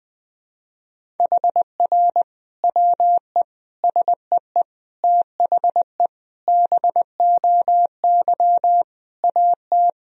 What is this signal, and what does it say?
Morse code, heard as a keyed tone, about 20 wpm, 700 Hz. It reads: HR WE SEE THE BOY AT
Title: Sentences from Top 300 Words and Abbreviations 20wpm